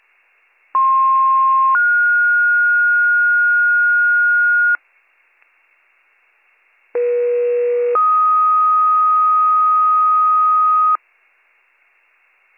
Two-tone page, heard as a one-second tone, then a three-second tone, about 1060 then 1430 Hz, about 480 then 1250 Hz. It is heard from 0.7 to 11.0 s.